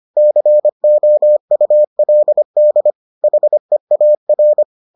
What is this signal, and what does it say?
Morse code, heard as a keyed tone, about 25 wpm, 600 Hz. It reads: COULD HEAR